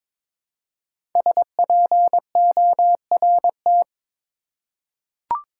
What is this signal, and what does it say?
Morse code, heard as a keyed tone, about 22 wpm, 700 Hz. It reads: SPORT
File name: 5 Letter Words - First 3 Letters Given - 22wpm